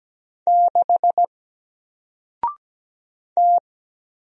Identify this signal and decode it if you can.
Morse code, heard as a keyed tone, about 17 wpm, 700 Hz. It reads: 6 T
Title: Single Character (Letter:Number:Punctuation:Prosign) Alt 17wpm